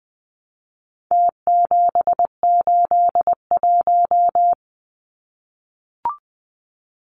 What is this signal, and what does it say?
Morse code, heard as a keyed tone, about 20 wpm, 700 Hz. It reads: T781
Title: Numbers - Contest Style - 4-Digit 20wpm